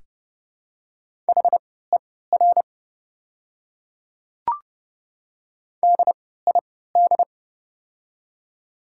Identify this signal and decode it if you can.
Morse code, heard as a keyed tone, about 30 wpm, spaced as for 17 wpm, 700 Hz. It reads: HER DID
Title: 3x Character Spacing - Top 100 Words - 30wpm